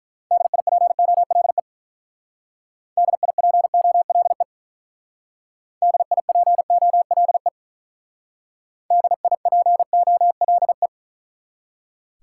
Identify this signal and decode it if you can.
Morse code, heard as a keyed tone, about 43 wpm, 700 Hz. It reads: DIPOLE DIPOLE DIPOLE DIPOLE
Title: Speed-Racing Top 90 QSO Elements 35wpm